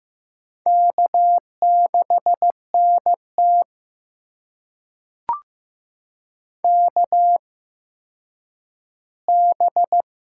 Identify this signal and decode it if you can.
Morse code, heard as a keyed tone, about 15 wpm, spaced as for 9 wpm, 700 Hz. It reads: K6NT KB
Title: USA Call Signs - 8x Character Spacing - 15wpm